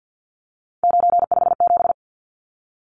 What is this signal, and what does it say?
Morse code, heard as a keyed tone, about 50 wpm, 700 Hz. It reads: 957
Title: Numbers - 3-Digit 50wpm